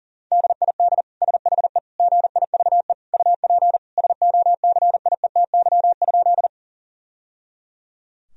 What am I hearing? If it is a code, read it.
Morse code, heard as a keyed tone, about 40 wpm, 700 Hz. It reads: DID SHE GIVE UP SOCIETY?
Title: Sentences from Top 1000 Words - Review Part 4 - 40wpm